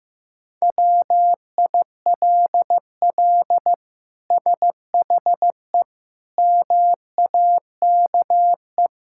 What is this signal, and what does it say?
Morse code, heard as a keyed tone, about 15 wpm, 700 Hz. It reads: WILL SHE MAKE
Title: Sentences from Top 1000 Words - Review Part 5 - 15wpm